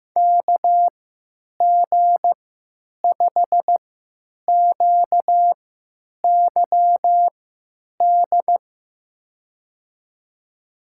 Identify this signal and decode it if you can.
Morse code, heard as a keyed tone, about 15 wpm, spaced as for 9 wpm, 700 Hz. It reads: KG5QYD
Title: USA Call Signs - 6 Characters - 3x Character Spacing - 15wpm